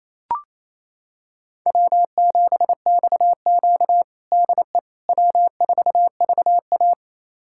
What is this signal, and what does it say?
Morse code, heard as a keyed tone, about 28 wpm, 700 Hz. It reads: W7XQ DE W4VA
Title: USA Callsign DE Callsign - 4 Characters - 28wpm